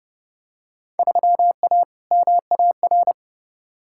Morse code, heard as a keyed tone, about 30 wpm, 700 Hz. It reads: 3A MAR
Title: Field Day Exchanges Encore - 30wpm